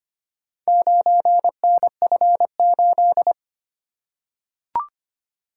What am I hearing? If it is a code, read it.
Morse code, heard as a keyed tone, about 25 wpm, 700 Hz. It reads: 9NF8
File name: practice / 25wpm / Four Letter-Number Combinations - 25wpm